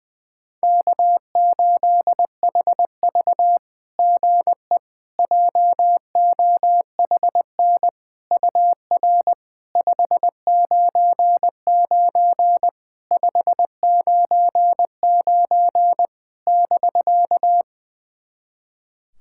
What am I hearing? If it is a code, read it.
Morse code, heard as a keyed tone, about 21 wpm, 700 Hz. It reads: K8HV GE JOHN UR 599 599 <BK>
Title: SOTA QSO Practice 20wpm